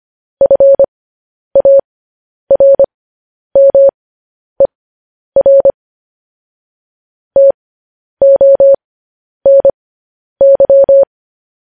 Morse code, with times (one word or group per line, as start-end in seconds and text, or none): 0.4-5.7 FARMER
7.4-11.0 TONY